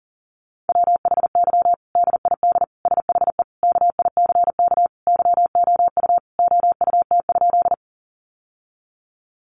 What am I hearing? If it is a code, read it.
Morse code, heard as a keyed tone, about 40 wpm, 700 Hz. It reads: WHY DID SHE KICK YOU OUT?
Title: Sentences from Top 1300 Words 40wpm